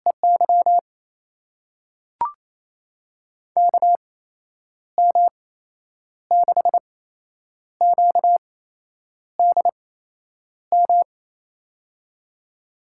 Morse code, heard as a keyed tone, about 28 wpm, spaced as for 9 wpm, 700 Hz. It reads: EY KM6QDM
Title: USA Call Signs - 6 Characters - 8x Character Spacing - 28wpm